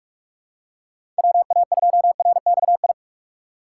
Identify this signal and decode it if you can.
Morse code, heard as a keyed tone, about 46 wpm, 700 Hz. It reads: WA2RXI
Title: USA Call Signs - 6 Characters - 3x Character Spacing - 45wpm